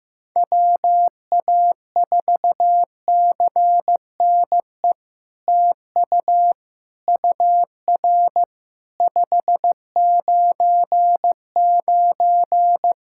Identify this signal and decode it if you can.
Morse code, heard as a keyed tone, about 15 wpm, 700 Hz. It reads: WA4CNE TU UR 599
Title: SOTA QSO Practice 15wpm